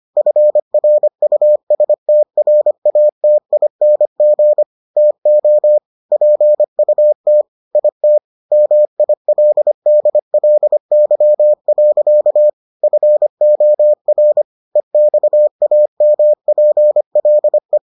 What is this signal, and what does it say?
Morse code, heard as a keyed tone, about 25 wpm, 600 Hz. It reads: FRUSTRATING TO PUT IT MILDLY. FOR EXAMPLE